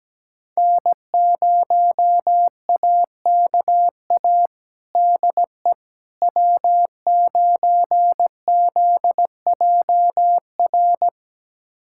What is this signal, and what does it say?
Morse code, heard as a keyed tone, about 17 wpm, 700 Hz. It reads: N0AKA DE W9ZJR
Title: USA Callsign DE Callsign - 5 Characters - 17wpm